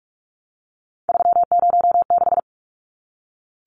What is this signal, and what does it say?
Morse code, heard as a keyed tone, about 44 wpm, 700 Hz. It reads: 306
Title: Numbers - 3-Digit 45wpm